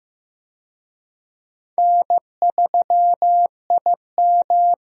Morse code, heard as a keyed tone, about 15 wpm, 700 Hz. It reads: N3IM
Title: USA Call Signs - 4 Characters - 15wpm